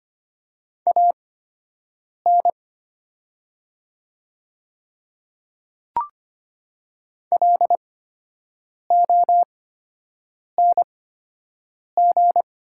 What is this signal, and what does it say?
Morse code, heard as a keyed tone, about 25 wpm, spaced as for 7 wpm, 700 Hz. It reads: AN LONG
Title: 8x Character Spacing - Top 100 Words - 25wpm